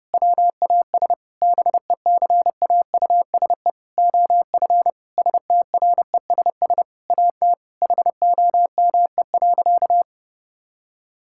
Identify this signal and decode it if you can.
Morse code, heard as a keyed tone, about 32 wpm, 700 Hz. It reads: WAS BECAUSE OF STRESS AT HOME.
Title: Sentences from Top 1000 Words - Review Part 6 - 30wpm